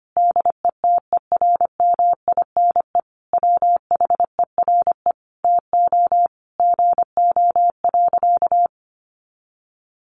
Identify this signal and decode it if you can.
Morse code, heard as a keyed tone, about 25 wpm, 700 Hz. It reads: DETERMINE WHERE TO GO.
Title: Sentences from Top 700 Words 25wpm